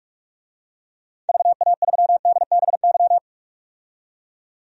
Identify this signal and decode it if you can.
Morse code, heard as a keyed tone, about 45 wpm, 700 Hz. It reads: VA3DBY
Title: DXCC-100 Call Signs - 6 Characters - 45wpm